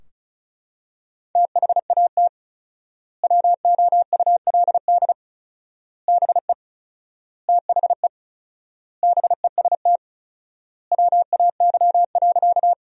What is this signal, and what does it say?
Morse code, heard as a keyed tone, about 36 wpm, 700 Hz. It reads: THAT WOULD BE THE BEST WAY.
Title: 4x Word-Spacing - Sentences from Top 400 Words - 35wpm